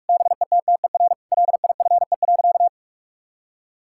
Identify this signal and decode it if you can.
Morse code, heard as a keyed tone, about 46 wpm, 700 Hz. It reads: BETTER LIFE.
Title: Sentences from Top 1000 Words - Review Part 5 - 45wpm